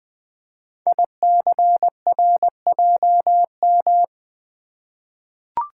Morse code, heard as a keyed tone, about 20 wpm, 700 Hz. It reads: ICRJM
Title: Code Groups - 5 Letters - 20wpm